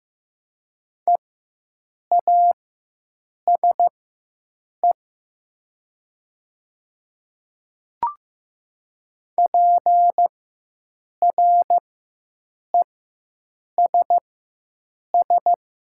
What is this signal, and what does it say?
Morse code, heard as a keyed tone, about 15 wpm, spaced as for 7 wpm, 700 Hz. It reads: EASE PRESS
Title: 4x Character Spacing - Top 500 Words - 15wpm